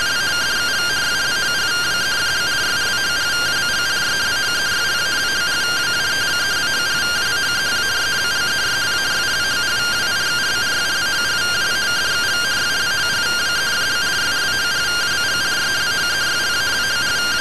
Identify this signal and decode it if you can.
Two-tone teleprinter signal, about 170 Hz shift.